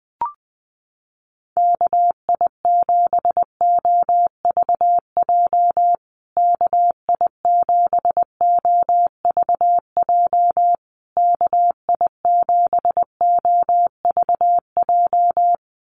Morse code, heard as a keyed tone, about 19 wpm, 700 Hz. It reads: KI7OVJ KI7OVJ KI7OVJ
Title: USA Call Signs Repeated 3x 20wpm